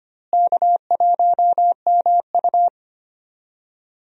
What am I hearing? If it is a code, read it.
Morse code, heard as a keyed tone, about 25 wpm, 700 Hz. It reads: K1MU
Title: USA Call Signs - 4 Characters - Repeated 3x - 25wpm